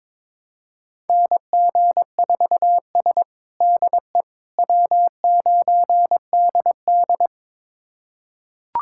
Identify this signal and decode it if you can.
Morse code, heard as a keyed tone, about 22 wpm, 700 Hz. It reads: NG4S DE W9DD
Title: USA Callsign DE Callsign - 4 Characters - 22wpm